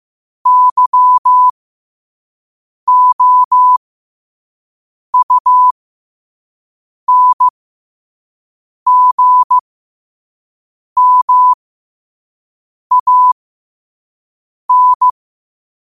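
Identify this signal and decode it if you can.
Morse code, heard as a keyed tone, about 15 wpm, spaced as for 5 wpm, 1000 Hz. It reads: YOUNGMAN
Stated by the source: Morse code practice at 5 wpm.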